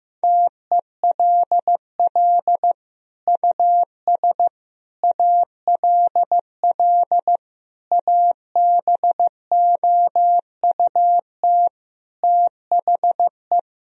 Morse code, heard as a keyed tone, about 15 wpm, 700 Hz. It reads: TELL US ALL ABOUT THE
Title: Sentences from Top 1000 Words - Review Part 4 - 15wpm